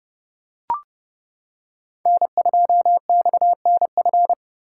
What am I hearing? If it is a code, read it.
Morse code, heard as a keyed tone, about 29 wpm, 700 Hz. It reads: N2XNF